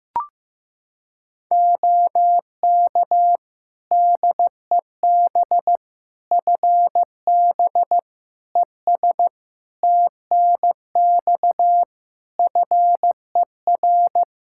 Morse code, heard as a keyed tone, about 15 wpm, 700 Hz. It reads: OK DEB FB ES TNX FER
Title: Practice QSO - Exchange 2 - Info - 15wpm